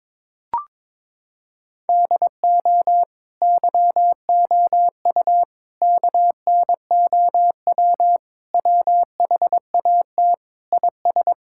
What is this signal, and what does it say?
Morse code, heard as a keyed tone, about 22 wpm, 700 Hz. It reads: DO YOU KNOW WHAT IS
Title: Sentences from Top 1000 Words - Review Part 5 - 22wpm